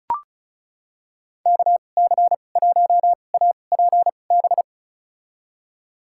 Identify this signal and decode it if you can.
Morse code, heard as a keyed tone, about 36 wpm, spaced as for 25 wpm, 700 Hz. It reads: KC1APB
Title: USA Call Signs - 2x Character Spacing - 35wpm